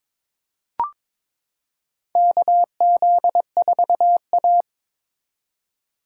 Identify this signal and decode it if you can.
Morse code, heard as a keyed tone, about 22 wpm, 700 Hz. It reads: KZ4A
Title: K1USN SST QSO Practice - 22wpm